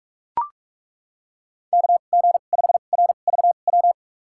Morse code, heard as a keyed tone, about 43 wpm, spaced as for 32 wpm, 700 Hz. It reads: KG5RVW
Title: USA Call Signs - 6 Characters - 2x Character Spacing - 45wpm